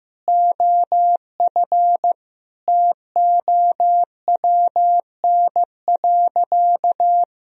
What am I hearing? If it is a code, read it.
Morse code, heard as a keyed tone, about 15 wpm, 700 Hz. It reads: OF TOWN.
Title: Sentences from Top 1000 Words - Review Part 2 - 15wpm